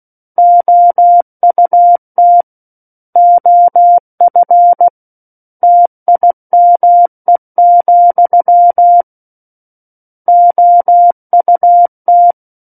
Morse code, heard as a keyed tone, about 16 wpm, 700 Hz. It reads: OUT OF TIME, OUT
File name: Book One / 07-Chapter 7 - 16 WPM